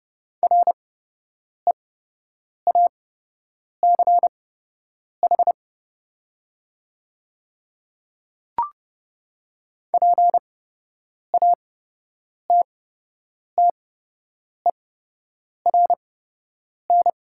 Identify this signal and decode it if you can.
Morse code, heard as a keyed tone, about 30 wpm, spaced as for 8 wpm, 700 Hz. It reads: REACH PATTERN
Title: 8x Character Spacing - Top 500 Words - 30wpm